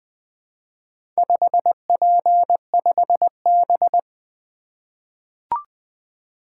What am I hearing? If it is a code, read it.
Morse code, heard as a keyed tone, about 20 wpm, 700 Hz. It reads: 5P5B
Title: Four Letter-Number Combinations - 20wpm